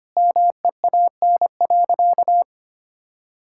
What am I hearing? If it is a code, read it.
Morse code, heard as a keyed tone, about 25 wpm, 700 Hz. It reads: MEAN.
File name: practice / 25wpm / Sentences with 2-Word Phrases - 25wpm